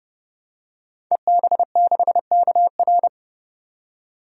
Morse code, heard as a keyed tone, about 30 wpm, 700 Hz. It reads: EB6KR
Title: DXCC-100 Call Signs - 5 Characters - 30wpm